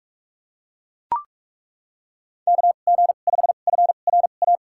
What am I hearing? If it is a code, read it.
Morse code, heard as a keyed tone, about 42 wpm, spaced as for 31 wpm, 700 Hz. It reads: KG5FRA
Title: USA Call Signs - 6 Characters - 2x Character Spacing - 45wpm